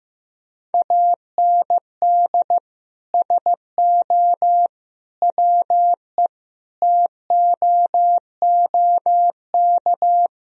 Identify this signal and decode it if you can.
Morse code, heard as a keyed tone, about 16 wpm, 700 Hz. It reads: AND SO WE TOOK
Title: Sentences from Top 1000 Words - Review Part 1 - 15wpm